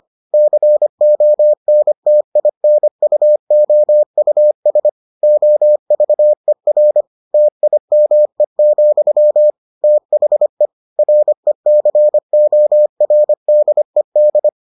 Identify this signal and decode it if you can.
Morse code, heard as a keyed tone, about 25 wpm, 600 Hz. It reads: CONTINUOUS OVER TIME, THE RECORDED